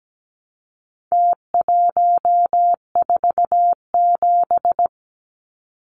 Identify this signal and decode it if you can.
Morse code, heard as a keyed tone, about 17 wpm, 700 Hz. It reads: T147
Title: Numbers - Contest Style - 4-Digit 17wpm